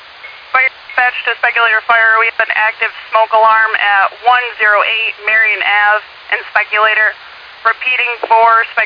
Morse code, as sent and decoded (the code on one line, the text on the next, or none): none